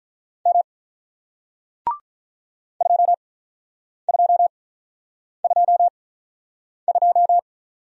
Morse code, heard as a keyed tone, about 48 wpm, spaced as for 9 wpm, 700 Hz. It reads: M 2222